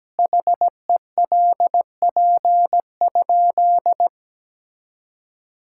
Morse code, heard as a keyed tone, about 17 wpm, 700 Hz. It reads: HELP?
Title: Sentences from Top 1000 Words - Review Part 5 - 17wpm